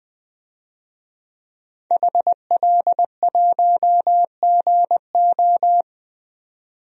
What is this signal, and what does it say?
Morse code, heard as a keyed tone, about 20 wpm, 700 Hz. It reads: HL1GO
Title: DXCC-100 Call Signs - 5 Characters - 20wpm